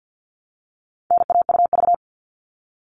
Morse code, heard as a keyed tone, about 50 wpm, 700 Hz. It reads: NAUV
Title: All Cut Numbers - 4-Digit 50wpm